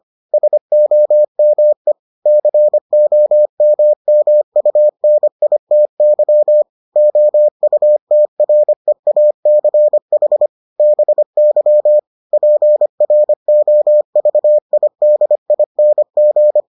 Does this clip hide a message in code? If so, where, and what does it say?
Yes: Morse code from 0.3 to 16.6 s, heard as a keyed tone, about 25 wpm, 600 Hz. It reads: SOME COMMUNITY OUTREACH BY PROVIDING